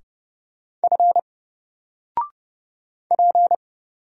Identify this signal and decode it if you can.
Morse code, heard as a keyed tone, about 30 wpm, 700 Hz. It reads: F P
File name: Single-Letter-Number-30wpm